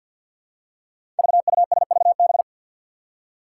Morse code, heard as a keyed tone, about 47 wpm, 700 Hz. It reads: VUSVB
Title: Five Challenging Letter-Number Combinations 50wpm